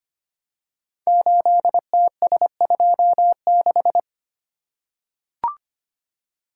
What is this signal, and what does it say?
Morse code, heard as a keyed tone, about 25 wpm, 700 Hz. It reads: 8TS26
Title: Code Groups - 5 Letter:Numbers - 25wpm